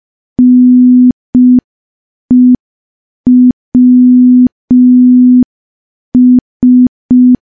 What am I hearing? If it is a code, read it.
Morse code, heard as a keyed tone, about 5 wpm, 250 Hz. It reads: NEWS
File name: morse / morse-training-05WPM